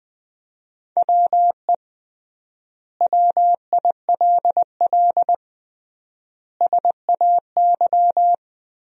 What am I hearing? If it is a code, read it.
Morse code, heard as a keyed tone, about 20 wpm, 700 Hz. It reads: WE WILL SAY
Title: 3x Word-Spacing - Sentences from Top 200 Words - 20wpm